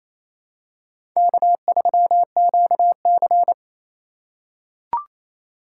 Morse code, heard as a keyed tone, about 27 wpm, 700 Hz. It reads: K3QC
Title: USA Call Signs - 2x Character Spacing - 28wpm